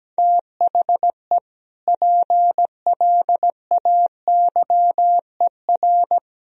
Morse code, heard as a keyed tone, about 17 wpm, 700 Hz. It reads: THE PLAYER
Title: Sentences from Top 1000 Words - Review Part 4 - 17wpm